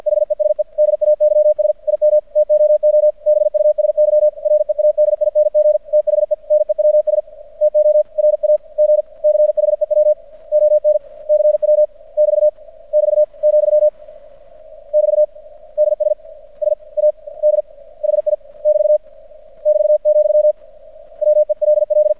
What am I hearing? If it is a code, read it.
Morse code, heard as a keyed tone, about 52 wpm, spaced as for 45 wpm, 600 Hz. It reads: HESE DAYS IM TOO BUSY READING THE NEWS TO RAG CHEW ON CW <BT> <BT> , <BT> DSSAD HI <BT> <BT>, WELL
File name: archived / CRSnet